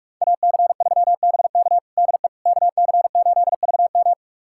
Morse code, heard as a keyed tone, about 46 wpm, 700 Hz. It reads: AC3BK DE KC8VM
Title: USA Callsign DE Callsign - 5 Characters - 45wpm